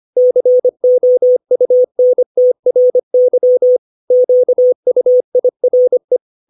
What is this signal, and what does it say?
Morse code, heard as a keyed tone, about 25 wpm, 500 Hz. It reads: COUNTRY QUIRE
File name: 632 words / 632 Words 25WPM